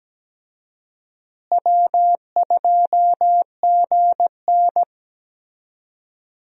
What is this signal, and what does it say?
Morse code, heard as a keyed tone, about 17 wpm, 700 Hz. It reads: W2GN
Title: USA Call Signs - 4 Characters - Repeated 2x - 17wpm